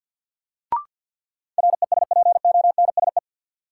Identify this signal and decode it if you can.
Morse code, heard as a keyed tone, about 52 wpm, 700 Hz. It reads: RESPONSE
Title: Sentences from Top 1400 Words - New Word Drill Alt - 50wpm